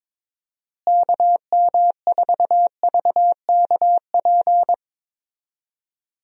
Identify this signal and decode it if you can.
Morse code, heard as a keyed tone, about 22 wpm, 700 Hz. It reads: KM4VKP
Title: USA Call Signs - 6 Characters - Repeated 2x - 22wpm